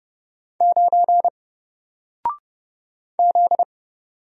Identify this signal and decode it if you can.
Morse code, heard as a keyed tone, about 30 wpm, 700 Hz. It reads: 9 Z
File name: Single-Letter-Number-30wpm